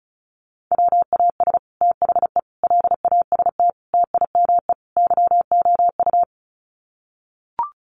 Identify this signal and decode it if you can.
Morse code, heard as a keyed tone, about 34 wpm, 700 Hz. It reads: WAS THE LAST TIME YOU
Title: Sets of 5 Words Encore2 35wpm